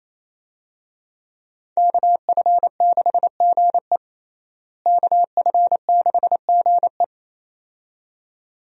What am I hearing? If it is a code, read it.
Morse code, heard as a keyed tone, about 28 wpm, 700 Hz. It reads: KF6GE KF6GE
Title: USA Call Signs - 5 Characters - Repeated 2x - 28wpm